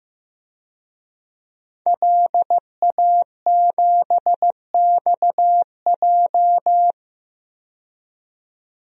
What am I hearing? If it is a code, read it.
Morse code, heard as a keyed tone, about 15 wpm, 700 Hz. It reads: LA7XJ
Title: DXCC-100 Call Signs - 5 Characters - 15wpm